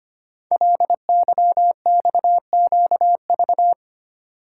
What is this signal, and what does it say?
Morse code, heard as a keyed tone, about 25 wpm, 700 Hz. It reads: LYXQV
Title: Code Groups - 5 Letters - 25wpm